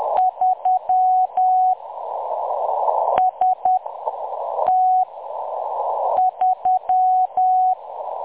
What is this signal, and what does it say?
Morse code, heard as a keyed tone, about 10 wpm, spaced as for 6 wpm, 740 Hz. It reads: 3 ST3
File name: M23-ST3- -0858utc-10184khz